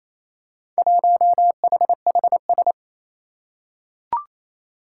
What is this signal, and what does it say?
Morse code, heard as a keyed tone, about 28 wpm, 700 Hz. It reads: 1HHS
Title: Four Challenging Letter-Number Combinations 28wpm